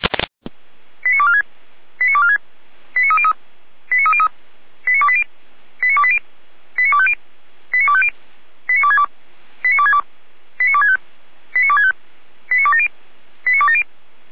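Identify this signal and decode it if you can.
Five-tone selective call (ZVEI), heard as quick runs of five tones, tones of about 70 ms. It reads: 89246 89246 89303 89303 89280 89280 89250 89250 89262 89262 89265 89265 89270 89270
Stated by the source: ZVEI five-tone selective call for address 89240, received by SDR at feuerwehr-orlamuende (Orlamünde fire brigade).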